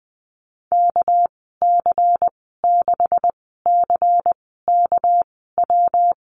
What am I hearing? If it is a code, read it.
Morse code, heard as a keyed tone, about 20 wpm, spaced as for 15 wpm, 700 Hz. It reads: KC6CKW
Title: 207 Lesson 33g - Letter j - Call Signs 20wpm